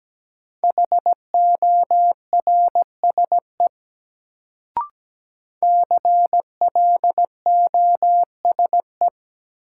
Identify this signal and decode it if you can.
Morse code, heard as a keyed tone, about 17 wpm, 700 Hz. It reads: HORSE CLOSE
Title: Sentences from Top 300 Words - New Word Drill - 17wpm